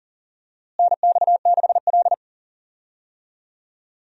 Morse code, heard as a keyed tone, about 39 wpm, 700 Hz. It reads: NX6L
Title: USA Call Signs - 4 Characters - Repeated 2x - 40wpm